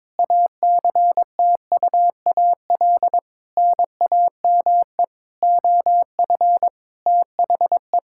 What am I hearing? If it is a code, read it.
Morse code, heard as a keyed tone, about 22 wpm, 700 Hz. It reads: ACTUAL NAME OF THE
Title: Sentences from Top 1000 Words - Review Part 5 - 22wpm